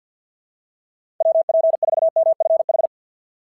Morse code, heard as a keyed tone, about 49 wpm, 650 Hz. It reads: WP4MFH
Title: SOTA QSO Practice 50wpm